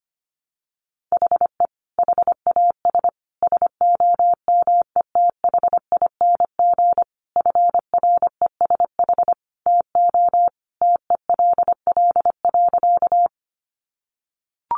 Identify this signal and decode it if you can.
Morse code, heard as a keyed tone, about 25 wpm, 700 Hz. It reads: HE HAS SOMETHING FRESH TO TELL.
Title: Sentences from Top 1200 Words 25wpm